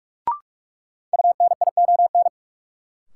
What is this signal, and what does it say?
Morse code, heard as a keyed tone, about 43 wpm, 700 Hz. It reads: UNION
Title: Sentences from Top 1400 Words - New Word Drill - 45wpm